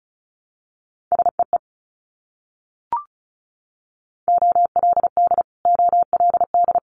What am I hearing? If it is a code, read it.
Morse code, heard as a keyed tone, about 36 wpm, 700 Hz. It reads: SEE OLD OLD
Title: Multi-letter Phonemes Repeated 3x 35wpm